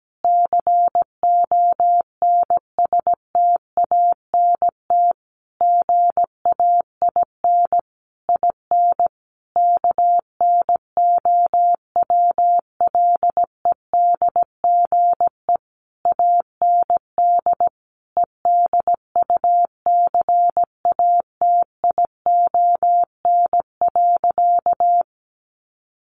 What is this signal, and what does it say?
Morse code, heard as a keyed tone, about 17 wpm, 700 Hz. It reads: CONSTANT GAIN IN KNOWLEDGE AND EDUCATION.
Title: Sentences from Top 1500 Words 17wpm